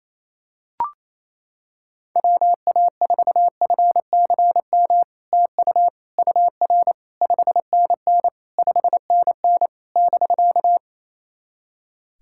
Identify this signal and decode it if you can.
Morse code, heard as a keyed tone, about 28 wpm, 700 Hz. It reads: WA4FCM TU UR 5NN 5NN <BK>
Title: SOTA QSO Practice 28wpm